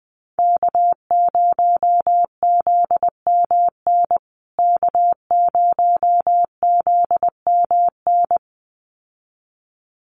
Morse code, heard as a keyed tone, about 20 wpm, 700 Hz. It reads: K0ZMN K0ZMN
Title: USA Call Signs Repeated 2x 20wpm